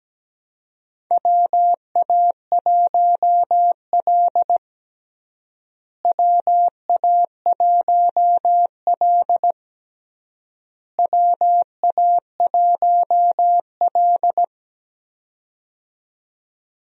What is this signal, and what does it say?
Morse code, heard as a keyed tone, about 17 wpm, 700 Hz. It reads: WA1L WA1L WA1L
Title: USA Call Signs - 4 Characters - Repeated 3x - 17wpm